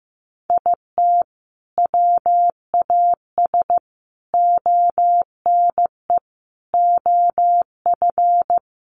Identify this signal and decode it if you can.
Morse code, heard as a keyed tone, about 15 wpm, 700 Hz. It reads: IT WAS ONE OF
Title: Sets of 5 Words 15wpm